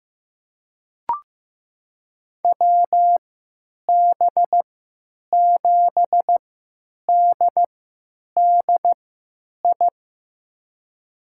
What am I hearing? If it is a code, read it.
Morse code, heard as a keyed tone, about 15 wpm, spaced as for 9 wpm, 700 Hz. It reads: WB7DDI
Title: USA Call Signs - 3x Character Spacing - 15wpm